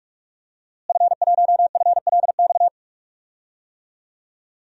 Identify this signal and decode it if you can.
Morse code, heard as a keyed tone, about 45 wpm, 700 Hz. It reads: F1FLX
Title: DXCC-100 Call Signs - 5 Characters - Repeated 3x - 45wpm